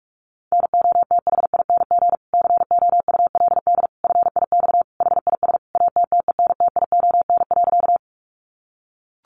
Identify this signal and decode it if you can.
Morse code, heard as a keyed tone, about 44 wpm, 700 Hz. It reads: NOTHING COULD FIX HIS ATTENTION.
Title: Sentences from Top 1400 Words 45wpm